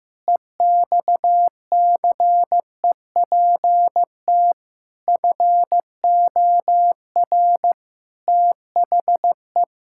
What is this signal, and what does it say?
Morse code, heard as a keyed tone, about 15 wpm, 700 Hz. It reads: EXCEPT FOR THE
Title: Pangram Challenge v2 - 15wpm